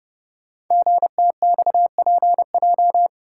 Morse code, heard as a keyed tone, about 30 wpm, 690 Hz. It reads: GTXPJ
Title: Code Groups - 5 Letters - 30wpm